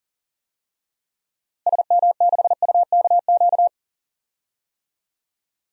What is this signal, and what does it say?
Morse code, heard as a keyed tone, about 39 wpm, 700 Hz. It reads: SM6UKQ
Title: DXCC-100 Call Signs - 6 Characters - 40wpm